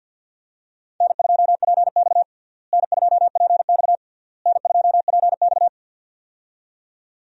Morse code, heard as a keyed tone, about 52 wpm, 700 Hz. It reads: N2PX N2PX N2PX
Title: USA Call Signs - 4 Characters - Repeated 3x - 50wpm